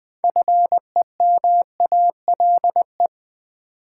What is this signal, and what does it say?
Morse code, heard as a keyed tone, about 20 wpm, 700 Hz. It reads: FEMALE
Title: 6 Letter Words - First 3 Letters Given - 20wpm